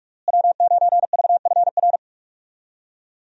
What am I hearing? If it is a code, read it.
Morse code, heard as a keyed tone, about 46 wpm, 700 Hz. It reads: W9VFR
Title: USA Call Signs - 5 Characters - Repeated 3x - 45wpm